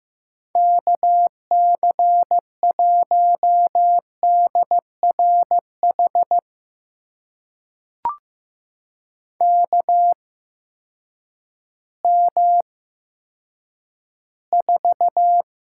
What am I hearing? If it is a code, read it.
Morse code, heard as a keyed tone, about 15 wpm, spaced as for 8 wpm, 700 Hz. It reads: KC1DRH KM4
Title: USA Call Signs - 6 Characters - 8x Character Spacing - 15wpm